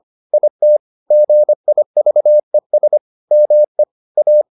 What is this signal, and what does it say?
Morse code, heard as a keyed tone, about 25 wpm, 600 Hz. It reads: IT GIVES ME A